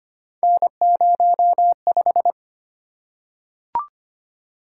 Morse code, heard as a keyed tone, about 25 wpm, 700 Hz. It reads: N05
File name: Three Letter-Number Combinations - 25wpm